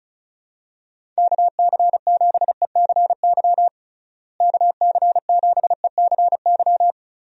Morse code, heard as a keyed tone, about 36 wpm, 700 Hz. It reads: KC7ECY KC7ECY
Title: USA Call Signs - 6 Characters - Repeated 3x - 35wpm